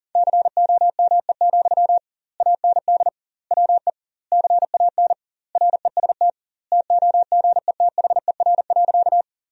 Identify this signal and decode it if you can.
Morse code, heard as a keyed tone, about 40 wpm, 700 Hz. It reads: COME, AND WE CAN REST TOGETHER.
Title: 2x Word-Spacing - Sentences from Top 500 Words - 40wpm